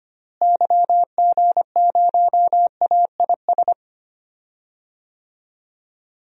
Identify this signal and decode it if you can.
Morse code, heard as a keyed tone, about 25 wpm, 700 Hz. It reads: YG0AIS
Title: DXCC-100 Call Signs - 6 Characters - Repeated 3x - 25wpm